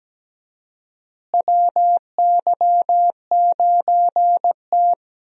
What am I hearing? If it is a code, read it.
Morse code, heard as a keyed tone, about 17 wpm, 700 Hz. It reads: WY9T
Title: USA Call Signs - 4 Characters - Repeated 3x - 17wpm